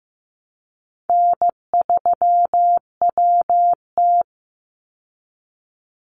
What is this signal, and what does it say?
Morse code, heard as a keyed tone, about 15 wpm, 700 Hz. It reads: N3WT
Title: Easy Call Signs Repeated 2x - 15wpm